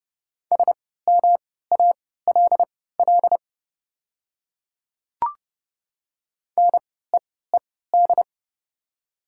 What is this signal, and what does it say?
Morse code, heard as a keyed tone, about 31 wpm, spaced as for 17 wpm, 690 Hz. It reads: SMALL NEED
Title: 3x Character Spacing - Top 200 Words - 30wpm